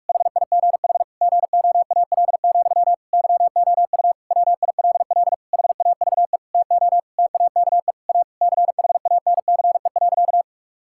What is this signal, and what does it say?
Morse code, heard as a keyed tone, about 46 wpm, 700 Hz. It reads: HIGH GOAL, YOU WILL HAVE TO TAKE A CHANCE.
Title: Sentences from Top 1000 Words - Review Part 6 - 45wpm